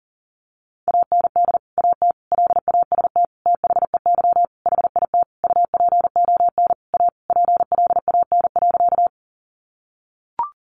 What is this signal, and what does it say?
Morse code, heard as a keyed tone, about 40 wpm, 700 Hz. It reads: AND AT LAST THEY HIT UPON A PLAN.
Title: Sentences from Top 1200 Words 40wpm